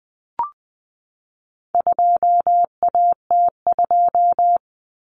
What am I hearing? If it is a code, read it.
Morse code, heard as a keyed tone, about 20 wpm, 700 Hz. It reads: 2AT2
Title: Common Cut Numbers - 4-Digit 20wpm